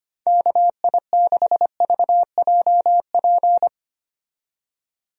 Morse code, heard as a keyed tone, about 25 wpm, 700 Hz. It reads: KI6VJP
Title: USA Call Signs - 6 Characters - 25wpm